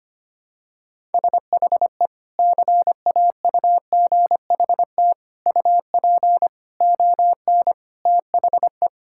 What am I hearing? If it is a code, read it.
Morse code, heard as a keyed tone, about 25 wpm, 700 Hz. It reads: SHE CAUGHT UP ON THE
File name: practice / 25wpm / Sentences with 3-Word Phrases - 25wpm